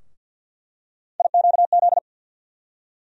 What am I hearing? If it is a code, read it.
Morse code, heard as a keyed tone, about 48 wpm, 700 Hz. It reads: IQZ
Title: USA Call Sign Postfixes - 3 Letters 50wpm